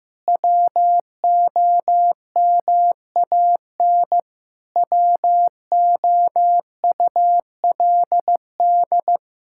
Morse code, heard as a keyed tone, about 15 wpm, 700 Hz. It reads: WOMAN WOULD
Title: Sentences from Top 1600 Words 15wpm